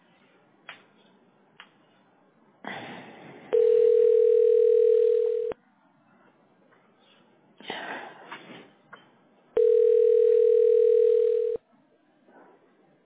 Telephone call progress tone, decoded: ringback tone